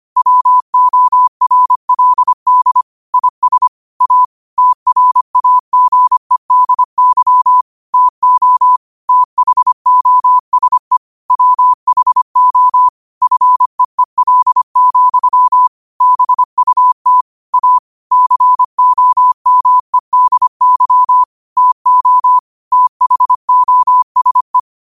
Morse code, heard as a keyed tone, about 25 wpm, 1000 Hz. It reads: WORLD IS A TRAGEDY TO THOSE WHO FEEL, BUT A COMEDY TO THOSE